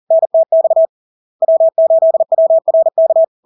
Morse code, heard as a keyed tone, about 40 wpm, 650 Hz. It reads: NTX W8WRK
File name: LARK-FD-2022-Log-in-CW-40wpm-650-Hz